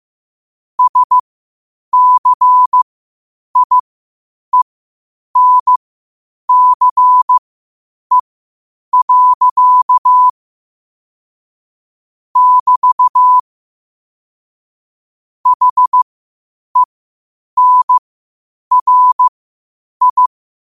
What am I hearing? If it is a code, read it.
Morse code, heard as a keyed tone, about 15 wpm, spaced as for 8 wpm, 1000 Hz. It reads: SCIENCE. <BT> HENRI